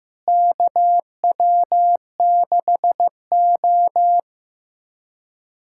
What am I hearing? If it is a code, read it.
Morse code, heard as a keyed tone, about 15 wpm, 700 Hz. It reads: KW6O
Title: USA Callsign DE Callsign - 4 Characters - 15wpm